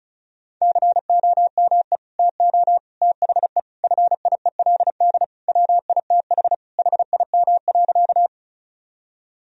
Morse code, heard as a keyed tone, about 36 wpm, 700 Hz. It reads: COME TO THE FIELD WITH HIM.
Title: Sentences from Top 1000 Words - Review Part 2 - 35wpm